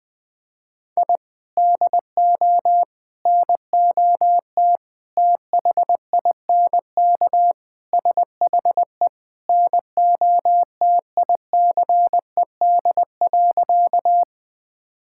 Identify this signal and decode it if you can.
Morse code, heard as a keyed tone, about 21 wpm, 700 Hz. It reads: I DO NOT THINK SHE NOTICED.
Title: Sentences with 5-Word Phrases - 20wpm